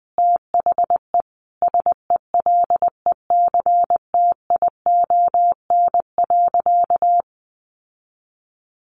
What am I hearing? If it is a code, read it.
Morse code, heard as a keyed tone, about 20 wpm, 700 Hz. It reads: THE SELECTION.
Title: Sentences from Top 1400 Words 20wpm